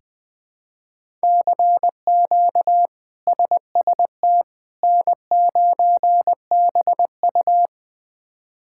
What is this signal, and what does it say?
Morse code, heard as a keyed tone, about 20 wpm, 700 Hz. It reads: CQ SST N9BU
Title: K1USN SST QSO Practice - 20wpm